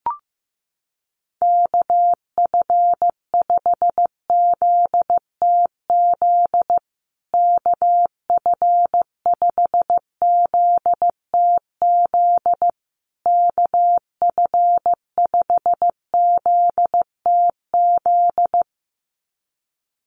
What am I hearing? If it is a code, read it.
Morse code, heard as a keyed tone, about 15 wpm, 700 Hz. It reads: KF5ZTZ KF5ZTZ KF5ZTZ
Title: USA Call Signs Repeated 3x 15wpm